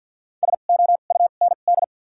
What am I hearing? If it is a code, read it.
Morse code, heard as a keyed tone, about 53 wpm, spaced as for 37 wpm, 690 Hz. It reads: SOUND
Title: 2x Character Spacing - Top 100 Words - 50wpm